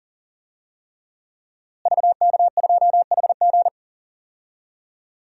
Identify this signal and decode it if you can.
Morse code, heard as a keyed tone, about 40 wpm, 700 Hz. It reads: VK2HG
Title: DXCC-100 Call Signs - 5 Characters - Repeated 3x - 40wpm